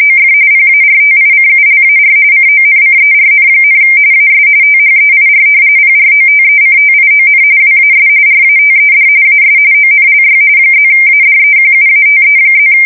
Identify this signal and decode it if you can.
RTTY, about 45.45 baud. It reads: OF PARTICULAR CONCERN, AS THEIR PROTECTION WAS RATED AS ONLY +ADEQUATE+
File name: fsk170 / battleships